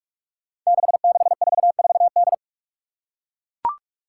Morse code, heard as a keyed tone, about 44 wpm, 700 Hz. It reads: 6644D
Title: Five Challenging Letter-Number Combinations 45wpm